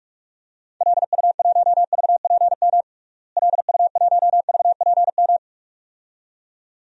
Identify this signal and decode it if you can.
Morse code, heard as a keyed tone, about 44 wpm, 700 Hz. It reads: LU1VPM LU1VPM
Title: DXCC-100 Call Signs - 6 Characters - Repeated 2x - 45wpm